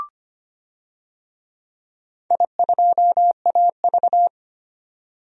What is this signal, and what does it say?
Morse code, heard as a keyed tone, about 25 wpm, 700 Hz. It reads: I2AV
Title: DXCC-100 Call Signs - 4 Characters - 25wpm